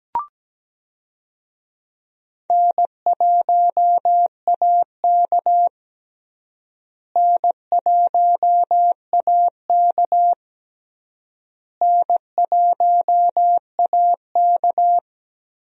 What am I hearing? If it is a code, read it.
Morse code, heard as a keyed tone, about 17 wpm, 700 Hz. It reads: N1AK N1AK N1AK
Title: USA Call Signs - 4 Characters - Repeated 3x - 17wpm